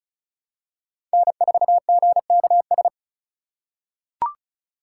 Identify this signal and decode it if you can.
Morse code, heard as a keyed tone, about 36 wpm, 700 Hz. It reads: N4GKS
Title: USA Call Signs - 4x Character Spacing - 35wpm